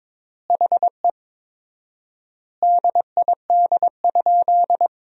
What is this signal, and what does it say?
Morse code, heard as a keyed tone, about 22 wpm, 700 Hz. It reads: HE DID?
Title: 4x Word-Spacing - Sentences from Top 100 Words - 22wpm